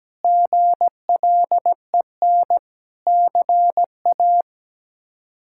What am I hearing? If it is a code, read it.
Morse code, heard as a keyed tone, about 17 wpm, 700 Hz. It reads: GLEN CA
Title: K1USN SST QSO Practice - 17wpm